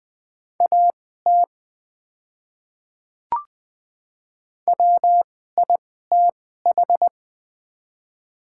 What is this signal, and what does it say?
Morse code, heard as a keyed tone, about 20 wpm, spaced as for 14 wpm, 700 Hz. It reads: AT WITH
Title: 2x Character Spacing - Top 100 Words - 20wpm